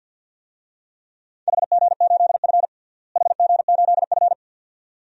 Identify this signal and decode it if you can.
Morse code, heard as a keyed tone, about 48 wpm, 700 Hz. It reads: HG8F HG8F
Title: DXCC-100 Call Signs - 4 Characters - Repeated 2x - 50wpm